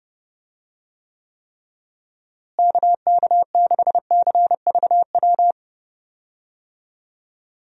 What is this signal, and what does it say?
Morse code, heard as a keyed tone, about 32 wpm, 700 Hz. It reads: KK6CVW
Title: USA Call Signs - 6 Characters - Repeated 3x - 30wpm